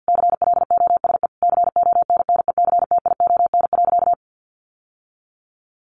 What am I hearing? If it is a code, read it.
Morse code, heard as a keyed tone, about 50 wpm, 700 Hz. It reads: CLOSE CONNECTION.
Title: Sentences from Top 1300 Words 50wpm